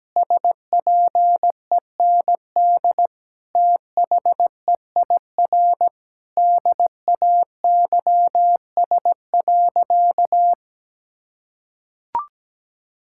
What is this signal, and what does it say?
Morse code, heard as a keyed tone, about 17 wpm, 700 Hz. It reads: SPEND THEIR DAYS.
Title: Sentences from Top 1000 Words - Review Part 5 - 17wpm